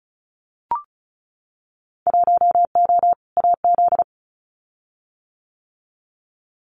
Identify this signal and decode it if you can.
Morse code, heard as a keyed tone, about 36 wpm, 690 Hz. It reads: 1O AZ
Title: Winter Field Day Exchanges 35wpm